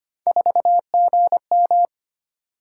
Morse code, heard as a keyed tone, about 25 wpm, 700 Hz. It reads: 4GM
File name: Three Letter-Number Combinations - 25wpm